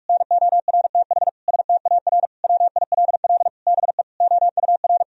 Morse code, heard as a keyed tone, about 46 wpm, 700 Hz. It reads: NORTH STAR WILL BE OUR
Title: Sentences from Top 1000 Words - Review Part 3 - 45wpm